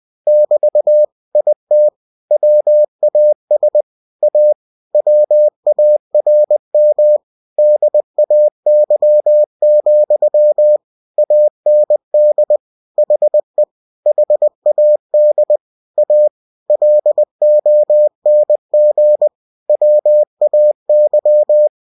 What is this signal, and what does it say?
Morse code, heard as a keyed tone, about 20 wpm, 600 Hz. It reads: <BT> IT WAS A WARM DAY, AND HE HAD A LONG WAY
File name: cw-books / winnie-the-pooh / winnie-20wpm-page-0049